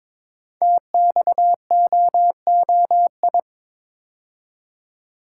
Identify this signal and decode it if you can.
Morse code, heard as a keyed tone, about 22 wpm, 700 Hz. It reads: TXOOI
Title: Code Groups - 5 Letters - 22wpm